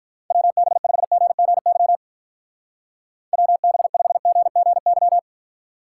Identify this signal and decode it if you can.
Morse code, heard as a keyed tone, about 44 wpm, 700 Hz. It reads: WB5GGY WB5GGY
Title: Speed Racing - USA 2x3 Call Signs 35wpm